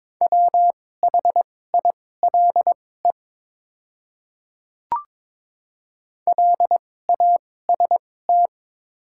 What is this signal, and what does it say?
Morse code, heard as a keyed tone, about 22 wpm, spaced as for 16 wpm, 700 Hz. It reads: WHILE LAST
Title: 2x Character Spacing - Top 500 Words - 22wpm